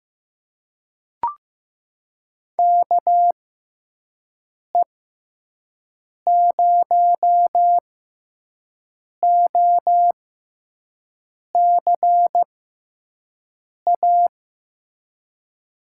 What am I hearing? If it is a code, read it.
Morse code, heard as a keyed tone, about 15 wpm, spaced as for 5 wpm, 700 Hz. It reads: KE0OCA
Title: USA Call Signs - 6 Characters - 6x Character Spacing - 15wpm